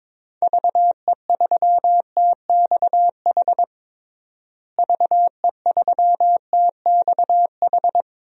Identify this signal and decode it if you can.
Morse code, heard as a keyed tone, about 22 wpm, 700 Hz. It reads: VE3TXH VE3TXH
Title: DXCC-100 Call Signs - 6 Characters - Repeated 3x - 22wpm